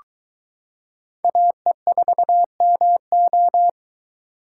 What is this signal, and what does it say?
Morse code, heard as a keyed tone, about 23 wpm, 700 Hz. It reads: AE4MO